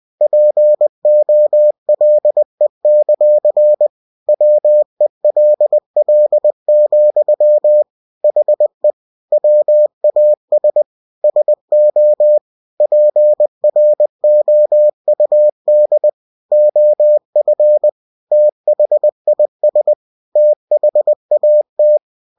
Morse code, heard as a keyed tone, about 20 wpm, 600 Hz. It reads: POLE; WELL, HE WAS SO PROUD OF THIS THAT
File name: cw-books / winnie-the-pooh / winnie-20wpm-page-0093